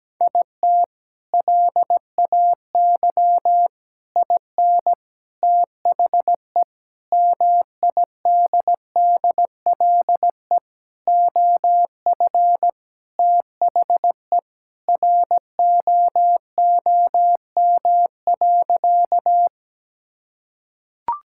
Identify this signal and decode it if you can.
Morse code, heard as a keyed tone, about 17 wpm, 700 Hz. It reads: IT LAY IN THE MIDDLE OF THE ROOM.
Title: Sentences from Top 1000 Words - Review Part 6 - 17wpm